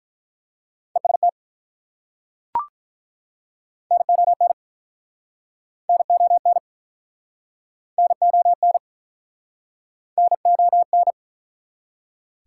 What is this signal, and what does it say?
Morse code, heard as a keyed tone, about 51 wpm, 700 Hz. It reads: EST NON NON NON NON